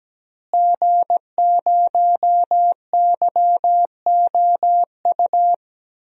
Morse code, heard as a keyed tone, about 17 wpm, 700 Hz. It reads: G0YOU
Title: DXCC-100 Call Signs - 5 Characters - Repeated 2x - 17wpm